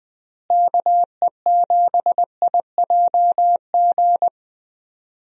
Morse code, heard as a keyed tone, about 20 wpm, 700 Hz. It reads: KE7IJG
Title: USA Call Signs - 6 Characters - Repeated 2x - 20wpm